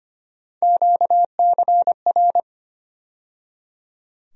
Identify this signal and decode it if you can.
Morse code, heard as a keyed tone, about 25 wpm, 700 Hz. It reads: QCR
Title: USA Call Sign Postfixes - 3 Letters 25wpm